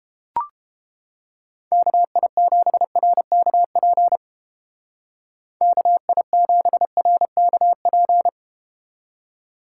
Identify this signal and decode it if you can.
Morse code, heard as a keyed tone, about 32 wpm, 700 Hz. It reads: KI7RKP KI7RKP